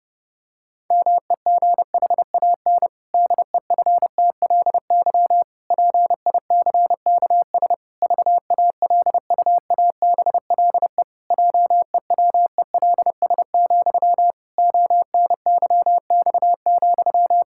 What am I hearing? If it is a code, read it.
Morse code, heard as a keyed tone, about 30 wpm, 700 Hz. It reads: MEGHAN DEFTLY PICKS VALUABLE JEWELS, ONYX,
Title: Pangram Challenge v2 - 30wpm